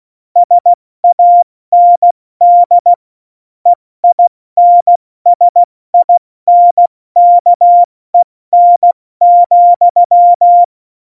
Morse code, heard as a keyed tone, about 16 wpm, 700 Hz. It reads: SAND EINSINKEN,
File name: Morsekurs / Witze / Witz 06. November 2018, 70 F80